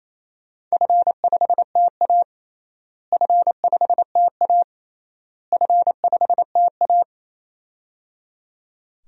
Morse code, heard as a keyed tone, about 28 wpm, 700 Hz. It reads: F5TA F5TA F5TA
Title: DXCC-100 Call Signs - 4 Characters - Repeated 3x - 28wpm